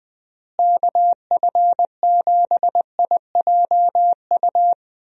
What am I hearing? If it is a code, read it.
Morse code, heard as a keyed tone, about 20 wpm, 700 Hz. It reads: KF7IJU
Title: USA Call Signs - 6 Characters - Repeated 3x - 20wpm